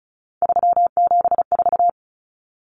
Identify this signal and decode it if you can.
Morse code, heard as a keyed tone, about 34 wpm, 700 Hz. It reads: 374
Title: Numbers - 3-Digit 35wpm